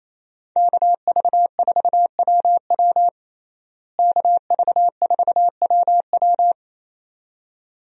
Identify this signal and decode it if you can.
Morse code, heard as a keyed tone, about 28 wpm, 700 Hz. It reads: KV4WW KV4WW
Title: USA Call Signs - 5 Characters - Repeated 2x - 28wpm